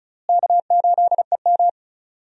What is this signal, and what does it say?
Morse code, heard as a keyed tone, about 34 wpm, 700 Hz. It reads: K8EM
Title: USA Call Signs - 4 Characters - Repeated 3x - 35wpm